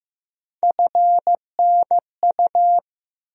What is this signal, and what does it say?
Morse code, heard as a keyed tone, about 15 wpm, 700 Hz. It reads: FNU
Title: USA Call Sign Postfixes - 3 Letters 15wpm